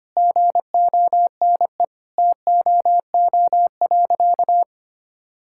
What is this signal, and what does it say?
Morse code, heard as a keyed tone, about 25 wpm, 700 Hz. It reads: GONE TOO.
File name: practice / 25wpm / Sentences with 2-Word Phrases - 25wpm